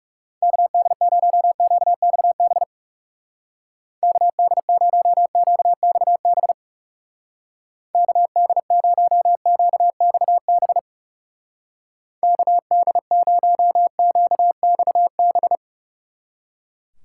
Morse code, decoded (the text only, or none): KD0QXB KD0QXB KD0QXB KD0QXB